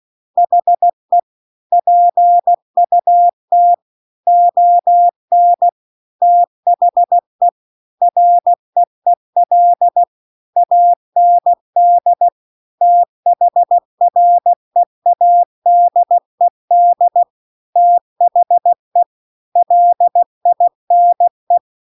Morse code, heard as a keyed tone, about 16 wpm, 700 Hz. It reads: HE PUT ON THE REEL AND THREADED THE LINE
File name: CW-The-Big-Twoheaded-River-2-16WPM